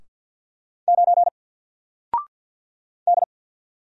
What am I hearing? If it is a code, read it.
Morse code, heard as a keyed tone, about 48 wpm, 700 Hz. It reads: 9 D
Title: Single Letter-Number Alt 50wpm